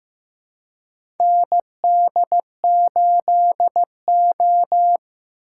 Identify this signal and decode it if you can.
Morse code, heard as a keyed tone, about 15 wpm, 700 Hz. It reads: ND8O
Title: USA Call Signs - 4 Characters - 4x Character Spacing - 15wpm